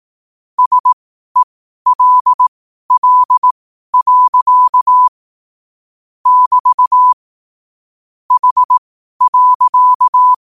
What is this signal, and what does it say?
Morse code, heard as a keyed tone, about 18 wpm, spaced as for 12 wpm, 1000 Hz. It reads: SELL. <BT> H.
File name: qotdfiles / 12wpm